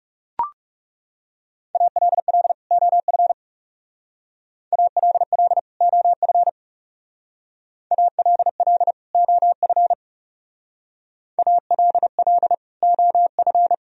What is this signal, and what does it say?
Morse code, heard as a keyed tone, about 39 wpm, 700 Hz. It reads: ALL OF ALL OF ALL OF ALL OF